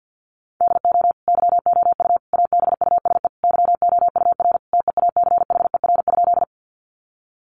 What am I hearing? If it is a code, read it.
Morse code, heard as a keyed tone, about 50 wpm, 700 Hz. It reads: DO YOU ABUSE YOUR TEACHER?
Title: Sentences from Top 1300 Words 50wpm